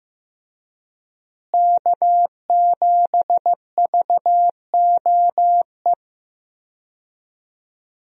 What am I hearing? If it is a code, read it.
Morse code, heard as a keyed tone, about 15 wpm, 700 Hz. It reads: K7VOE
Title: USA Call Signs - 5 Characters - Repeated 2x - 15wpm